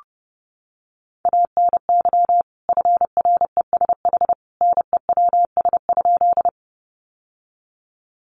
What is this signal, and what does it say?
Morse code, heard as a keyed tone, about 30 wpm, 700 Hz. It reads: ANY FRESH NEWS?
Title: Sentences from Top 1200 Words 30wpm